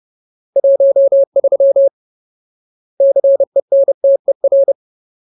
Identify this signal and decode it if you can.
Morse code, heard as a keyed tone, about 30 wpm, 550 Hz. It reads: 13 CENTER
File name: Ultimate Word List / 30WPM Ultimate